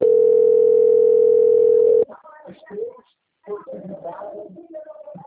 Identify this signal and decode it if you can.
Ringback tone (North America), 440 + 480 Hz, with speech.
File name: out- - - - - .5481